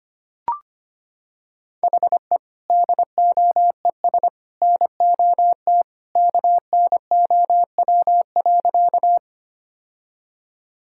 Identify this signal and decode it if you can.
Morse code, heard as a keyed tone, about 25 wpm, 700 Hz. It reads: HE DOES NOT KNOW.
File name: practice / 25wpm / Sentences with 3-Word Phrases - 25wpm